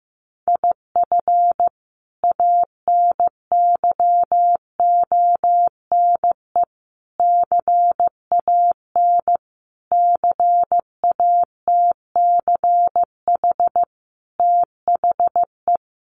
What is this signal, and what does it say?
Morse code, heard as a keyed tone, about 15 wpm, 700 Hz. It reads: IF ANYONE CAN CATCH THE
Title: Sentences from Top 1500 Words 15wpm